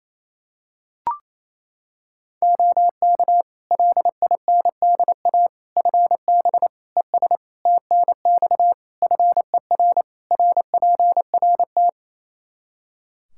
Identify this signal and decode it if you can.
Morse code, heard as a keyed tone, about 27 wpm, 700 Hz. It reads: OK LINDA FB ES TNX FER RPRT
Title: Practice QSO - Exchange 2 - Info - 28wpm